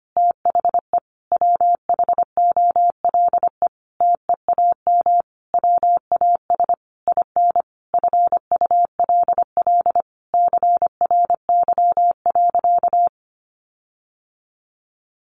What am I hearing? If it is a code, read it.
Morse code, heard as a keyed tone, about 25 wpm, 700 Hz. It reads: THE WHOLE TEAM WAS IN FULL CRY.
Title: Sentences from Top 1500 Words 25wpm